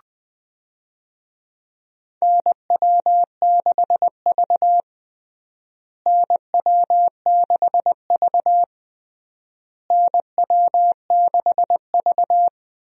Morse code, heard as a keyed tone, about 21 wpm, 700 Hz. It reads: NW6V NW6V NW6V
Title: USA Call Signs - 4 Characters - Repeated 3x - 20wpm